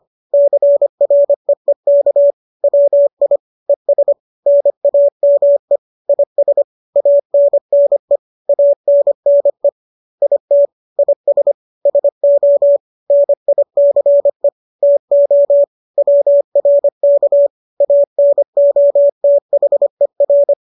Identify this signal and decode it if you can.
Morse code, heard as a keyed tone, about 25 wpm, 570 Hz. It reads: CREEK WI ES NAME IS ANNE ANNE IT IS SO NICE TO WRK ANOTHER